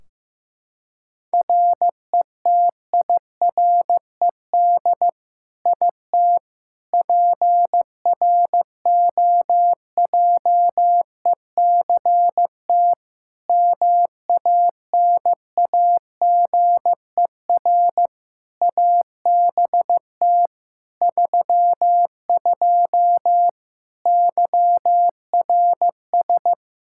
Morse code, heard as a keyed tone, about 15 wpm, 700 Hz. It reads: RETIRED IT PROJECT MANAGER ABT 32 YRS
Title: Practice QSO - Exchange 3 - More Info - 15wpm